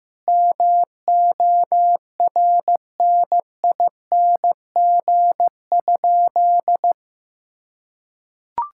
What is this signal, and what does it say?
Morse code, heard as a keyed tone, about 15 wpm, 700 Hz. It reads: MORNING?
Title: Sentences from Top 1600 Words 15wpm